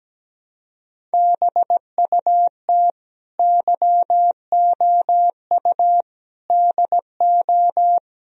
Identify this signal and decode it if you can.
Morse code, heard as a keyed tone, about 18 wpm, 700 Hz. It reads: BUT YOU DO
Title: Speed-Racing - Sets of 4 Words 17wpm